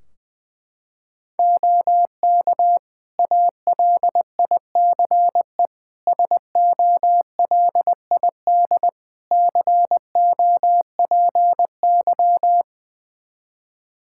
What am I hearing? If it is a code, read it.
Morse code, heard as a keyed tone, about 20 wpm, 700 Hz. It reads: OK ALICE SOLID COPY
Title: Practice QSO - Exchange 3 - More Info - 20wpm